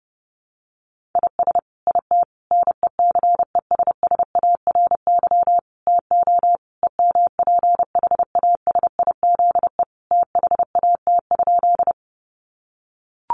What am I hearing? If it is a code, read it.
Morse code, heard as a keyed tone, about 30 wpm, 700 Hz. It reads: IS IT NECESSARY TO EMPHASIZE THAT?
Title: Sentences from Top 1500 Words 30wpm